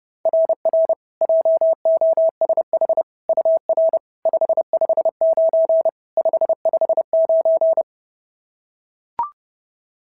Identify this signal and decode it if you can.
Morse code, heard as a keyed tone, about 30 wpm, 690 Hz. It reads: RR JOSH UR 559 559
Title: SOTA QSO Practice 30wpm